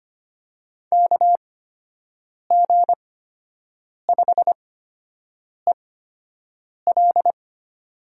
Morse code, heard as a keyed tone, about 25 wpm, spaced as for 7 wpm, 700 Hz. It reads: KG5EL
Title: USA Call Signs - 6 Characters - 8x Character Spacing - 25wpm